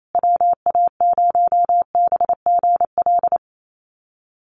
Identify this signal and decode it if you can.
Morse code, heard as a keyed tone, about 28 wpm, 700 Hz. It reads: WA0BGL
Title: USA Call Signs Repeated 3x 28wpm